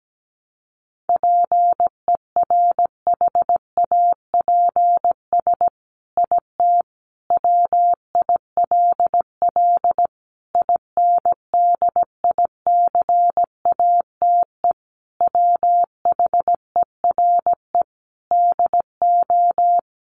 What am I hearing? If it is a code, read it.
Morse code, heard as a keyed tone, about 17 wpm, 700 Hz. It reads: PERHAPS IT WILL INDICATE WHERE DO
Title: Sentences from Top 1300 Words 17wpm